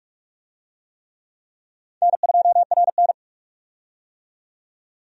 Morse code, heard as a keyed tone, about 45 wpm, 700 Hz. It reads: N2RN
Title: USA Call Signs - 4 Characters - 45wpm